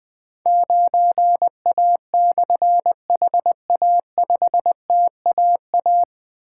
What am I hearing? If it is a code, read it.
Morse code, heard as a keyed tone, about 20 wpm, 700 Hz. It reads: 9A/HA5TAA
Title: Difficult Call Signs 20wpm